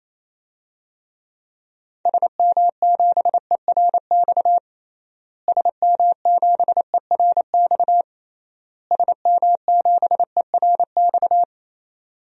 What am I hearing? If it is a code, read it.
Morse code, heard as a keyed tone, about 28 wpm, 700 Hz. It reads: SM7ERX SM7ERX SM7ERX
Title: DXCC-100 Call Signs - 6 Characters - Repeated 3x - 28wpm